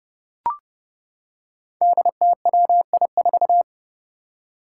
Morse code, heard as a keyed tone, about 29 wpm, 700 Hz. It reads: DTWI4
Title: Five Challenging Letter-Number Combinations 30wpm